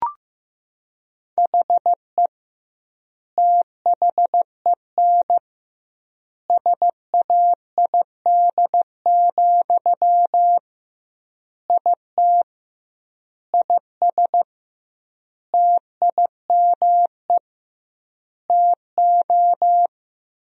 Morse code, heard as a keyed tone, about 15 wpm, 700 Hz. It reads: HE THEN SAID, IT IS TIME TO
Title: 2x Word-Spacing - Sentences from Top 300 Words - 15wpm